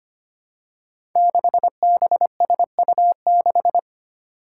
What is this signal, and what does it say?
Morse code, heard as a keyed tone, about 25 wpm, 700 Hz. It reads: 6BSU6
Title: Five Challenging Letter-Number Combinations 25wpm